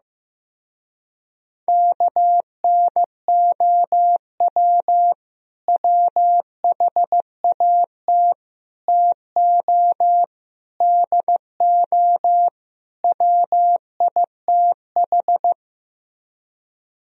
Morse code, heard as a keyed tone, about 15 wpm, 700 Hz. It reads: KNOW WHAT TO DO WITH
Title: Speed-Racing - Sets of 5 Words 15wpm